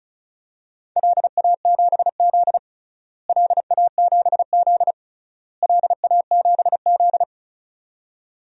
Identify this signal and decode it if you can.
Morse code, heard as a keyed tone, about 36 wpm, 700 Hz. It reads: LA7Z LA7Z LA7Z
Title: DXCC-100 Call Signs - 4 Characters - Repeated 3x - 35wpm